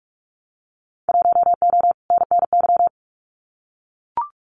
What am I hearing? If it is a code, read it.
Morse code, heard as a keyed tone, about 44 wpm, 700 Hz. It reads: 1O NNY
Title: Winter Field Day Exchanges 45wpm